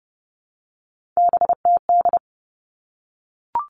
Morse code, heard as a keyed tone, about 30 wpm, 700 Hz. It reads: BTD
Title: All Cut Numbers - 3-Digit 30wpm